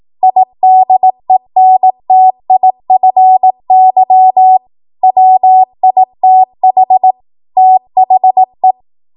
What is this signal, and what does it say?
Morse code, heard as a keyed tone, about 18 wpm, 750 Hz. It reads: IDENTIFY WITH THE
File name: Morse / 18WPM